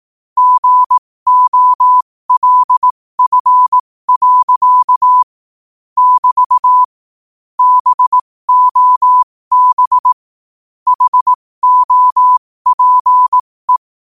Morse code, heard as a keyed tone, about 18 wpm, spaced as for 15 wpm, 1000 Hz. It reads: GOLF. <BT> BOB HOPE
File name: qotdfiles / 15wpm